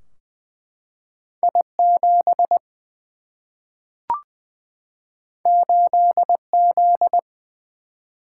Morse code, heard as a keyed tone, about 20 wpm, 700 Hz. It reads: I 7 8 Z
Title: Single-Letter Single-Number 20wpm